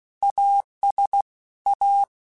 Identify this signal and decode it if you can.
Morse code, heard as a keyed tone, about 16 wpm, 800 Hz. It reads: AS A